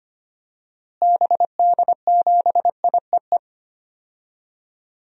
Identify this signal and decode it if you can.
Morse code, heard as a keyed tone, about 25 wpm, 700 Hz. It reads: BD7IEE
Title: DXCC-100 Call Signs - 6 Characters - Repeated 2x - 25wpm